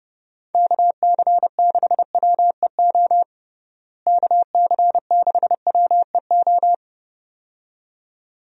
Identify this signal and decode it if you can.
Morse code, heard as a keyed tone, about 30 wpm, 700 Hz. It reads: KC6WEO KC6WEO
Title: USA Call Signs - 6 Characters - Repeated 2x - 30wpm